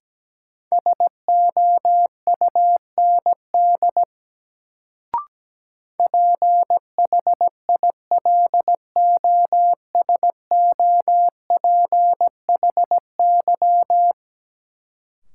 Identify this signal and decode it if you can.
Morse code, heard as a keyed tone, about 17 wpm, 700 Hz. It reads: SOUND PHILOSOPHY
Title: Sentences from Top 900 Words - New Word Drill - 17wpm